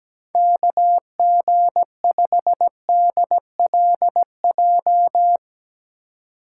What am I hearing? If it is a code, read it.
Morse code, heard as a keyed tone, about 17 wpm, 700 Hz. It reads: KG5DLJ
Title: USA Call Signs - 6 Characters - 2x Character Spacing - 17wpm